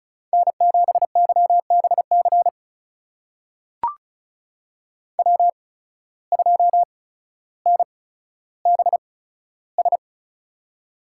Morse code, heard as a keyed tone, about 36 wpm, spaced as for 15 wpm, 700 Hz. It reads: N7YBC W2NBS
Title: USA Call Signs - 5 Characters - 8x Character Spacing - 35wpm